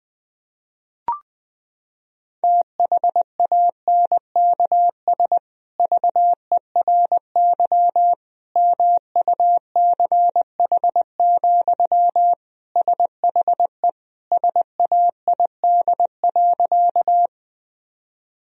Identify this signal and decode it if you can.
Morse code, heard as a keyed tone, about 21 wpm, 700 Hz. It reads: THANKS VERY MUCH, SHE SAID.
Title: Sentences from Top 1000 Words - Review Part 3 - 20wpm